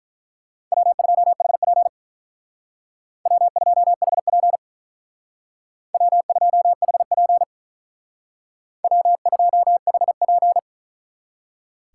Morse code, heard as a keyed tone, about 43 wpm, 700 Hz. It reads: W2HP W2HP W2HP W2HP
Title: Speed Racing - USA Call Signs 35wpm